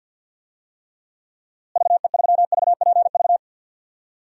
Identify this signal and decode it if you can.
Morse code, heard as a keyed tone, about 53 wpm, 700 Hz. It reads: VE3VPV
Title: DXCC-100 Call Signs - 6 Characters - Repeated 3x - 50wpm